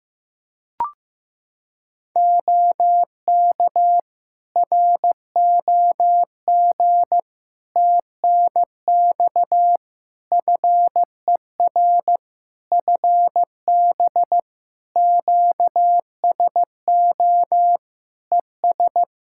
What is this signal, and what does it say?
Morse code, heard as a keyed tone, about 16 wpm, 700 Hz. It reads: OK ROG TNX FER FB QSO ES
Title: Practice QSO - Exchange 4 - Ending - 15wpm